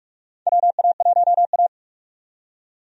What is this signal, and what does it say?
Morse code, heard as a keyed tone, about 43 wpm, 700 Hz. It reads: WA1A
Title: USA Call Signs - 4 Characters - 45wpm